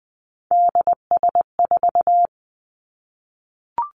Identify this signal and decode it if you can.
Morse code, heard as a keyed tone, about 20 wpm, 700 Hz. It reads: DS4
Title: Three Challenging Letter-Number Combinations 20wpm